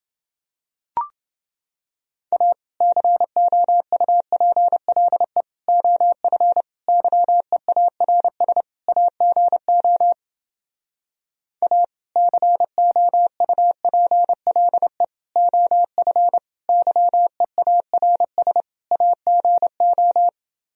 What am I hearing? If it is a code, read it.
Morse code, heard as a keyed tone, about 30 wpm, 700 Hz. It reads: A COUPLE OF YEARS AGO A COUPLE OF YEARS AGO